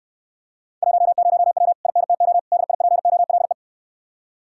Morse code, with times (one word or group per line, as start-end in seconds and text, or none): 0.8-1.7 10W
1.9-2.4 INTO
2.5-3.5 DIPOLE